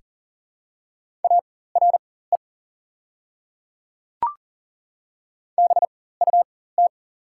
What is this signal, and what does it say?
Morse code, heard as a keyed tone, about 40 wpm, spaced as for 19 wpm, 690 Hz. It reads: ARE BUT
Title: 4x Character Spacing - Top 100 Words - 40wpm